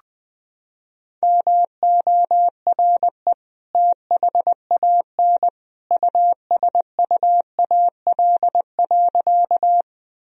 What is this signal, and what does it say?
Morse code, heard as a keyed tone, about 20 wpm, 700 Hz. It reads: MORE THAN USUAL.
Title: Sentences from Top 1000 Words - Review Part 2 - 20wpm